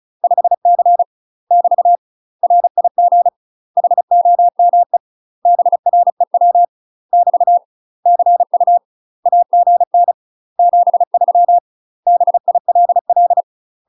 Morse code, heard as a keyed tone, about 34 wpm, 700 Hz. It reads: 5C <BT> RIG HOME BREW <BT> CU AGN 73 BILL
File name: UK MASTER QSOs TRAINING / 35WPM / UK MASTER QSOs 0037